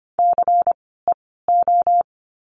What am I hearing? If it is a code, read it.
Morse code, heard as a keyed tone, about 25 wpm, spaced as for 16 wpm, 700 Hz. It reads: CEO
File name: course / 25:10wpm / 094 Lesson 16f - Letter c - Words 25wpm